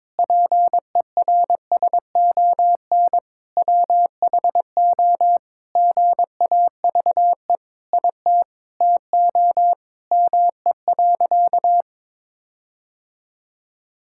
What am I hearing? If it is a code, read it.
Morse code, heard as a keyed tone, about 22 wpm, 700 Hz. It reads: PERSON WHO GAVE IT TO ME.
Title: Sentences from Top 1600 Words 22wpm